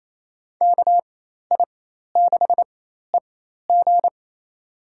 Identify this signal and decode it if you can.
Morse code, heard as a keyed tone, about 28 wpm, spaced as for 13 wpm, 700 Hz. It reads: KI6EG
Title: USA Call Signs - 4x Character Spacing - 28wpm